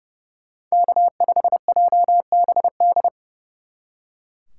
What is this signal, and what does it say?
Morse code, heard as a keyed tone, about 36 wpm, 700 Hz. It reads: K5JBD
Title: Speed Racing - USA Call Signs 30wpm